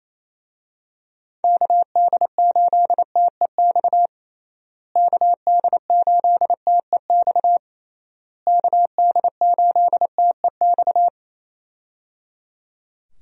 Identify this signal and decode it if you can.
Morse code, heard as a keyed tone, about 27 wpm, 700 Hz. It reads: KD8TEX KD8TEX KD8TEX
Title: USA Call Signs - 6 Characters - Repeated 3x - 28wpm